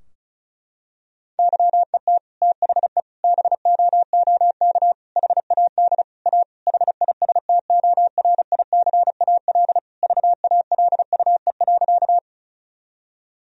Morse code, heard as a keyed tone, about 34 wpm, 700 Hz. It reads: YET THE BOOK HAD A HISTORICAL VALUE.
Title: Sentences from Top 1000 Words - Review Part 4 - 35wpm